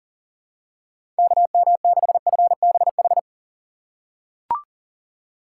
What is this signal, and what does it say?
Morse code, heard as a keyed tone, about 40 wpm, 700 Hz. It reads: KM6FBH
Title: USA Call Signs - 6 Characters - 4x Character Spacing - 40wpm